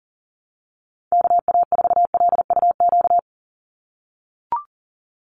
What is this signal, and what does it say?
Morse code, heard as a keyed tone, about 37 wpm, 700 Hz. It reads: KA4LUQ
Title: USA Call Signs 40wpm